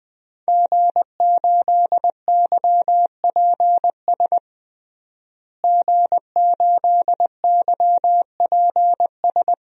Morse code, heard as a keyed tone, about 21 wpm, 700 Hz. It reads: G8YPS G8YPS
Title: DXCC-100 Call Signs - 5 Characters - Repeated 2x - 20wpm